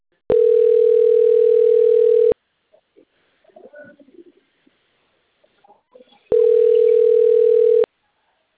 Ringback tone (North America), 440 + 480 Hz.